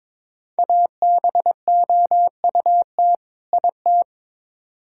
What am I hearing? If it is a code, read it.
Morse code, heard as a keyed tone, about 22 wpm, 700 Hz. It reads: ABOUT IT
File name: Build A Phrase - 2-Word Phrases - 22wpm